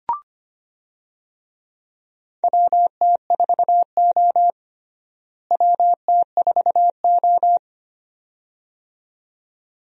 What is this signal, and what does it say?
Morse code, heard as a keyed tone, about 25 wpm, 700 Hz. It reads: WT4O WT4O
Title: USA Call Signs - 4 Characters - Repeated 2x - 25wpm